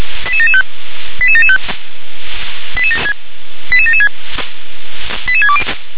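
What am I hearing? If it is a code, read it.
Five-tone selective call (ZVEI), heard as quick runs of five tones, tones of about 70 ms. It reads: 88775 88775 88776 88620
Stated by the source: ZVEI five-tone call received by SDR, code 88773.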